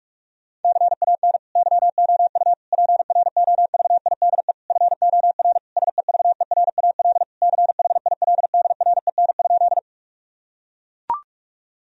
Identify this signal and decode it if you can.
Morse code, heard as a keyed tone, about 41 wpm, 700 Hz. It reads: CAN YOU PROVIDE FOR SEVERAL CHILDREN?
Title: Sentences from Top 1000 Words - Review Part 2 - 45wpm